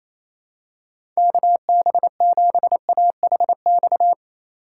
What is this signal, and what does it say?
Morse code, heard as a keyed tone, about 28 wpm, 700 Hz. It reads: KB7AHX
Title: USA Call Signs - 6 Characters - 3x Character Spacing - 28wpm